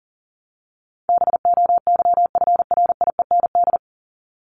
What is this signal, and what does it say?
Morse code, heard as a keyed tone, about 40 wpm, 700 Hz. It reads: BOYFRIEND
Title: Top 501-750 Words 40wpm